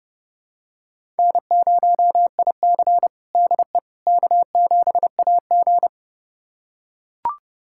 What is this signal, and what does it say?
Morse code, heard as a keyed tone, about 30 wpm, 700 Hz. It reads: N0IC DE K7AG
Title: USA Callsign DE Callsign - 4 Characters - 30wpm